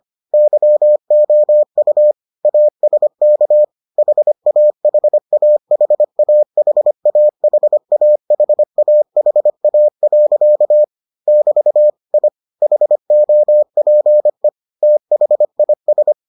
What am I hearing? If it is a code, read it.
Morse code, heard as a keyed tone, about 25 wpm, 600 Hz. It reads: YOU ASK HAHAHAHAHAHAHA. <BT> I HOPE THIS